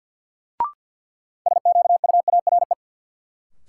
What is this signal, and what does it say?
Morse code, heard as a keyed tone, about 47 wpm, 700 Hz. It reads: SQUARE
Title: Sentences from Top 1000 Words - Review Part 7 - New Word Drill - 50wpm